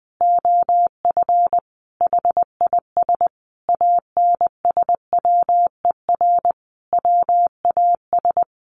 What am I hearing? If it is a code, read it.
Morse code, heard as a keyed tone, about 19 wpm, 700 Hz. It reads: OF HIS ANSWER WAS
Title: Sentences from Top 1500 Words 20wpm